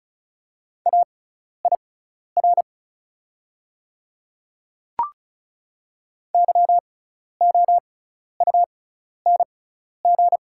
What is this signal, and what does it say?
Morse code, heard as a keyed tone, about 36 wpm, spaced as for 12 wpm, 700 Hz. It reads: AIR YOUNG
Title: 6x Character Spacing - Top 500 Words - 35wpm